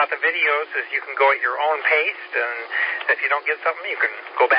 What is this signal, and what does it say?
Emergency Alert System (SAME) tones, heard as harsh buzzing bursts, with speech. It is heard from 0.7 to 4.2 s.